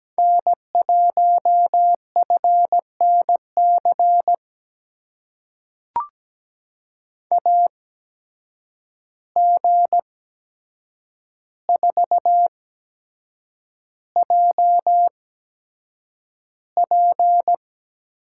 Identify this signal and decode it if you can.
Morse code, heard as a keyed tone, about 17 wpm, spaced as for 7 wpm, 700 Hz. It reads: N1FNC AG4JP
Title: USA Call Signs - 5 Characters - 8x Character Spacing - 17wpm